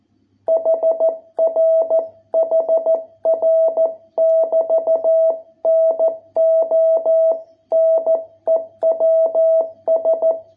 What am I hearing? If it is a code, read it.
Morse code, heard as a keyed tone, about 13 wpm, 660 Hz. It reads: HR HR <BT> NO NEWS